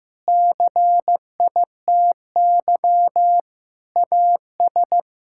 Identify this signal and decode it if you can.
Morse code, heard as a keyed tone, about 16 wpm, 700 Hz. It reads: CITY AS
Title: Pangram Challenge v2 - 15wpm